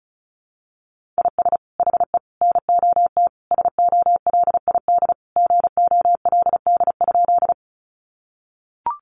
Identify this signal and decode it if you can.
Morse code, heard as a keyed tone, about 34 wpm, 700 Hz. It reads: IS HE NOT SOLID GOLD?
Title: Sentences from Top 1500 Words 35wpm